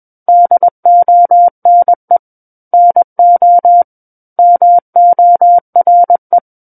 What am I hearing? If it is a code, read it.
Morse code, heard as a keyed tone, about 21 wpm, 700 Hz. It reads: DONE NO MORE
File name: Book One / 12-Chapter 12 - 21 WPM